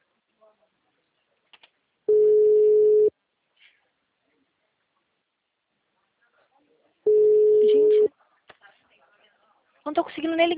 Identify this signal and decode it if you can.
Ringback tone (Europe, 425 Hz), with speech.
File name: out- -2010- - - .56766